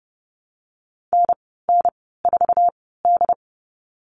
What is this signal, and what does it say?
Morse code, heard as a keyed tone, about 30 wpm, spaced as for 17 wpm, 700 Hz. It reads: NN4D
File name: Lesson 14g - Number 2 - Call Signs 30wpm